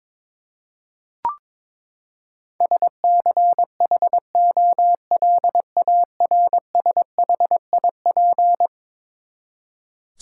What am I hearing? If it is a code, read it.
Morse code, heard as a keyed tone, about 23 wpm, 700 Hz. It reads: SCHOLARSHIP
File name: practice / 22wpm / Eleven Letter Words - 22wpm